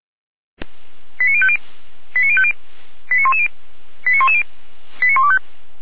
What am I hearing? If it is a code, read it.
Five-tone selective call (ZVEI), heard as quick runs of five tones, tones of about 70 ms. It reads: 89050 89050 89110 89110 89125